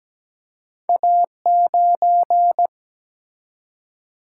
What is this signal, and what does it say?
Morse code, heard as a keyed tone, about 17 wpm, 700 Hz. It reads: A9
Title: Single-Letter Single-Number 17wpm